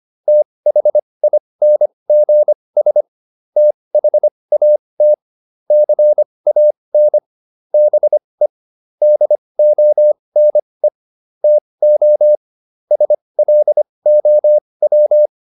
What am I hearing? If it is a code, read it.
Morse code, heard as a keyed tone, about 25 wpm, spaced as for 20 wpm, 600 Hz. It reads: THINGS THAT CAN BE DONE TO SLOW